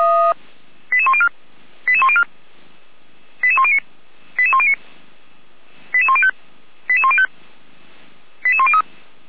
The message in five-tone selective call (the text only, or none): 88104 88104 88109 88109 88105 88105 88103